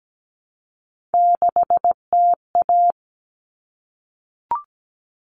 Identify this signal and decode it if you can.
Morse code, heard as a keyed tone, about 17 wpm, 700 Hz. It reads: 6TA
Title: All Cut Numbers - 3-Digit 17wpm